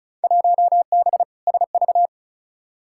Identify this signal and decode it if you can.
Morse code, heard as a keyed tone, about 34 wpm, 700 Hz. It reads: 1B SV
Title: Field Day Exchanges - 35wpm